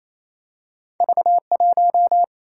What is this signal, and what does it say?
Morse code, heard as a keyed tone, about 28 wpm, 700 Hz. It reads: V1
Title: Single-Letter Single-Number 28wpm